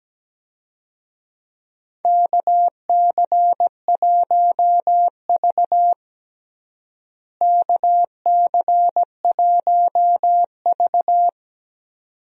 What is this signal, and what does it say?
Morse code, heard as a keyed tone, about 17 wpm, 700 Hz. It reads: KC1V KC1V
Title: USA Call Signs - 4 Characters - Repeated 2x - 17wpm